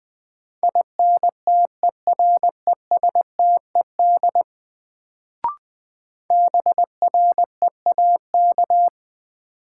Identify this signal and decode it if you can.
Morse code, heard as a keyed tone, about 20 wpm, 700 Hz. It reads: INTERESTED BREAK
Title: Sentences from Top 900 Words - New Word Drill - 20wpm